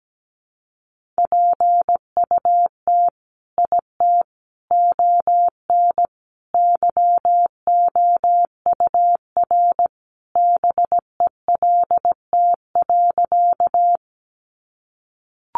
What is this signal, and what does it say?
Morse code, heard as a keyed tone, about 17 wpm, 700 Hz. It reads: PUT IT ON YOUR BELT.
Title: Sentences from Top 1400 Words 17wpm